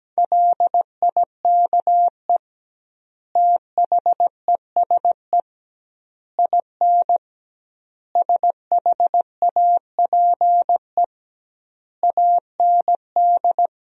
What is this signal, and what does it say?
Morse code, heard as a keyed tone, about 17 wpm, 700 Hz. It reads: LIKE THESE IN SHAPE AND
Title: 2x Word-Spacing - Sentences from Top 500 Words - 17wpm